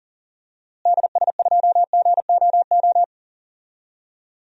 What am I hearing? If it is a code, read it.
Morse code, heard as a keyed tone, about 40 wpm, 700 Hz. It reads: DS2GOO
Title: DXCC-100 Call Signs - 6 Characters - Repeated 3x - 40wpm